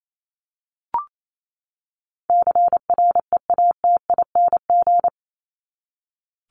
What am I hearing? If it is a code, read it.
Morse code, heard as a keyed tone, about 28 wpm, 700 Hz. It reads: CREATING
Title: ING Suffix - Top 100 Words with Base Words 28wpm